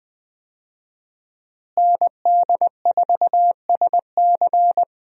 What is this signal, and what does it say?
Morse code, heard as a keyed tone, about 20 wpm, 700 Hz. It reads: ND4SC
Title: USA Call Signs - 5 Characters - Repeated 2x - 20wpm